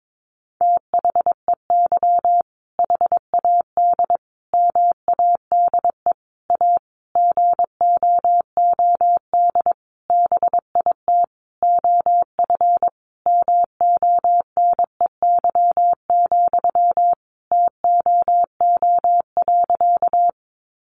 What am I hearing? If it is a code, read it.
Morse code, heard as a keyed tone, about 22 wpm, 700 Hz. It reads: THEY HAD MADE A GOOD BIT OF MONEY, TOO.
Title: Sentences from Top 900 Words 22wpm